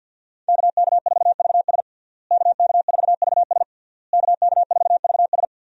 Morse code, heard as a keyed tone, about 51 wpm, 700 Hz. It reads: KK4VS KK4VS KK4VS
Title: USA Call Signs - 5 Characters - Repeated 3x - 50wpm